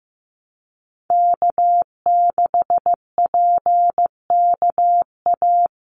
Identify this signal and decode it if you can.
Morse code, heard as a keyed tone, about 15 wpm, 700 Hz. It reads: K6PKA
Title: USA Call Signs 15wpm